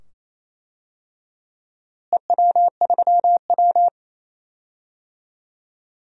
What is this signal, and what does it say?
Morse code, heard as a keyed tone, about 28 wpm, 700 Hz. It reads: EW3W
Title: DXCC-100 Call Signs - 4 Characters - Repeated 2x - 28wpm